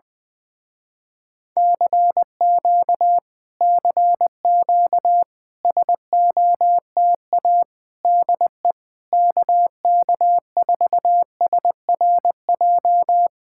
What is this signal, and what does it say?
Morse code, heard as a keyed tone, about 20 wpm, 700 Hz. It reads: CQ CQ SOTA DE KK4SRJ
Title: Speed-Racing Common QSO Phrases - 17wpm